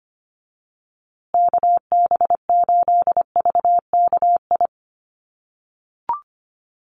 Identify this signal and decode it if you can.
Morse code, heard as a keyed tone, about 25 wpm, 700 Hz. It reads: KB8VKI
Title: USA Call Signs Repeated 3x 25wpm